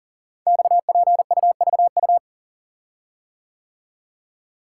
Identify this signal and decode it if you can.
Morse code, heard as a keyed tone, about 38 wpm, 690 Hz. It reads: XPUVU
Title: Code Groups - 5 Letters - 40wpm